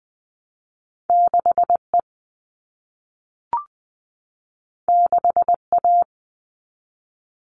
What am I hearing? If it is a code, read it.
Morse code, heard as a keyed tone, about 20 wpm, 700 Hz. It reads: 6E 6A
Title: All Cut Numbers - 2-Digit 20wpm